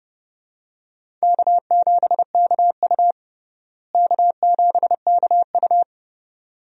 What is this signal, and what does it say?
Morse code, heard as a keyed tone, about 30 wpm, 700 Hz. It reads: K7KU K7KU
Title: USA Call Signs - 4 Characters - Repeated 2x - 30wpm